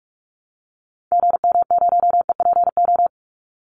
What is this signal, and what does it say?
Morse code, heard as a keyed tone, about 44 wpm, 700 Hz. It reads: GM0EPO
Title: Call Signs - 45wpm